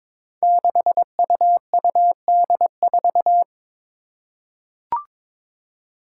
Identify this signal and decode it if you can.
Morse code, heard as a keyed tone, about 22 wpm, 700 Hz. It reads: 6UUD4
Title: Five Challenging Letter-Number Combinations 22wpm